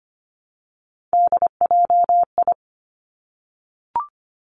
Morse code, heard as a keyed tone, about 25 wpm, 700 Hz. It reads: DJI
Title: Three Challenging Letter-Number Combinations 25wpm